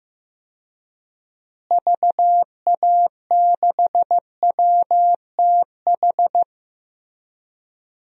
Morse code, heard as a keyed tone, about 15 wpm, 700 Hz. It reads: VA6WTH
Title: DXCC-100 Call Signs - 6 Characters - Repeated 2x - 15wpm